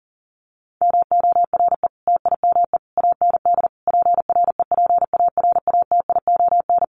Morse code, heard as a keyed tone, about 40 wpm, 700 Hz. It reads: MORE TIME AND PREPARATION
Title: Sentences from Top 1400 Words 40wpm